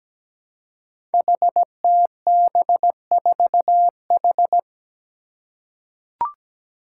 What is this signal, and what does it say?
Morse code, heard as a keyed tone, about 17 wpm, 700 Hz. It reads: HTB4H
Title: Five Challenging Letter-Number Combinations 17wpm